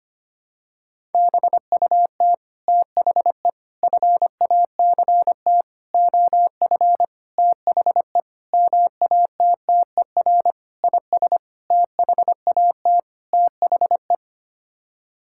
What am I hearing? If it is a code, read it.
Morse code, heard as a keyed tone, about 25 wpm, 700 Hz. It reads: BUT THE FACT OF THE MATTER IS THAT THE
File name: Sets of 9 Words - 25wpm